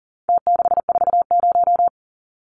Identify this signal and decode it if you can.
Morse code, heard as a keyed tone, about 40 wpm, 700 Hz. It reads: T640
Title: Numbers - Contest Style - 4-Digit 40wpm